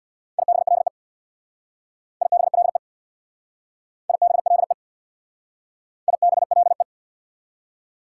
Morse code, heard as a keyed tone, about 60 wpm, 700 Hz. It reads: IBLE IBLE IBLE IBLE
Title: Speed-Racing Prefixes and Suffixes 50wpm